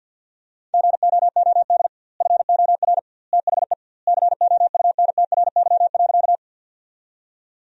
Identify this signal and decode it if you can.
Morse code, heard as a keyed tone, about 50 wpm, 700 Hz. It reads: GOOD FOR THE COUNTRY.
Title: 2x Word-Spacing - Sentences from Top 300 Words - 50wpm